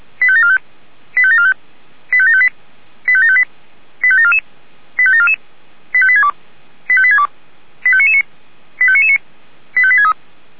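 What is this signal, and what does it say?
Five-tone selective call (ZVEI), heard as quick runs of five tones, tones of about 70 ms. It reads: 85745 85745 85758 85758 85744 85744 85862 85862 85999 85999 85863
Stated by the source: ZVEI five-tone call received on SDR by feuerwehr-orlamuende fire service.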